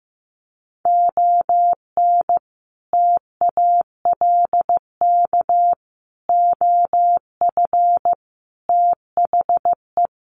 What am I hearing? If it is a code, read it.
Morse code, heard as a keyed tone, about 15 wpm, 700 Hz. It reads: ON TALK OF THE
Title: Sets of 5 Words 15wpm